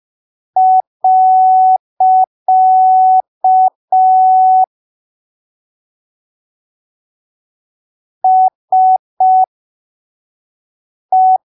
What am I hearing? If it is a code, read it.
Morse code, heard as a keyed tone, about 5 wpm, spaced as for 3 wpm, 750 Hz. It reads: . SE
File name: arquivos / texto / texto 20 05ppm